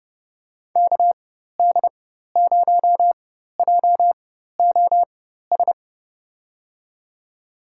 Morse code, heard as a keyed tone, about 29 wpm, spaced as for 14 wpm, 700 Hz. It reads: KD0JOS
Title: USA Call Signs - 4x Character Spacing - 30wpm